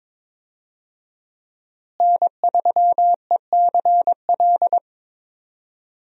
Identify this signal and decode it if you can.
Morse code, heard as a keyed tone, about 22 wpm, 700 Hz. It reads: N3ECL
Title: USA Call Signs - 5 Characters - Repeated 3x - 22wpm